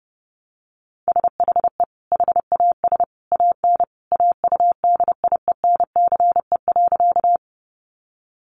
Morse code, heard as a keyed tone, about 30 wpm, 700 Hz. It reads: SHE HAS AN AUDIENCE.
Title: Sentences from Top 800 Words 30wpm